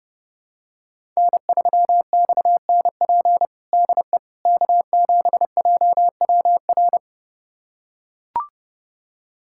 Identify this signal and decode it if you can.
Morse code, heard as a keyed tone, about 32 wpm, 700 Hz. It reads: N3XNP DE K7JWR
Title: USA Callsign DE Callsign - 5 Characters - 30wpm